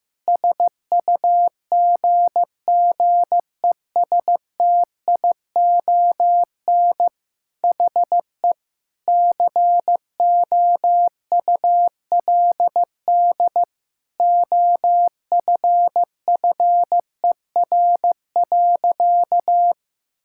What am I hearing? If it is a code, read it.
Morse code, heard as a keyed tone, about 15 wpm, 700 Hz. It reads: SUGGESTION HE COULD OFFER.
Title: Sentences from Top 1600 Words 15wpm